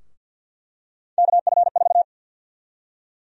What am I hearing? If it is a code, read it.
Morse code, heard as a keyed tone, about 53 wpm, 700 Hz. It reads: KF4
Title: USA Call Sign Prefixes - 50wpm